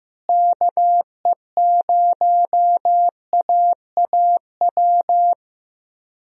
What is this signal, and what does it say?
Morse code, heard as a keyed tone, about 15 wpm, 700 Hz. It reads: KE0AAW
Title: USA Call Signs - 6 Characters - Repeated 2x - 15wpm